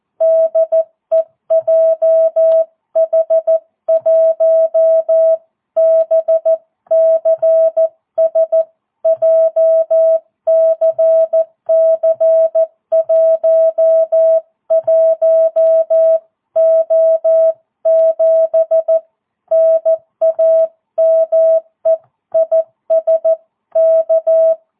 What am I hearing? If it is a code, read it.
Morse code, heard as a keyed tone, about 14 wpm, 650 Hz. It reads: DE JH1BCS JCC11O7 NAME IS K
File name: jh8cbh / 2022cw